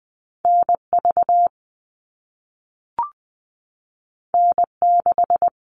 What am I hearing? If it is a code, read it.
Morse code, heard as a keyed tone, about 20 wpm, 700 Hz. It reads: N V N 6
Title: All Cut Numbers - 2-Digit 20wpm